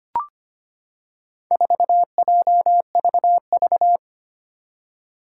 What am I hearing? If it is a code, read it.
Morse code, heard as a keyed tone, about 25 wpm, 700 Hz. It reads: 4JVV
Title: Four Challenging Letter-Number Combinations 25wpm